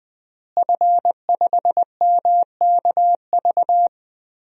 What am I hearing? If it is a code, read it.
Morse code, heard as a keyed tone, about 20 wpm, 700 Hz. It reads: F5MKV
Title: DXCC-100 Call Signs - 5 Characters - Repeated 3x - 20wpm